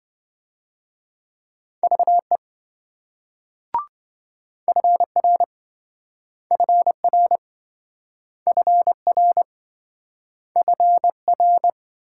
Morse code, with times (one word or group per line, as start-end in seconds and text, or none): 1.8-2.4 VE
4.7-5.4 FR
6.5-7.4 FR
8.5-9.4 FR
10.6-11.7 FR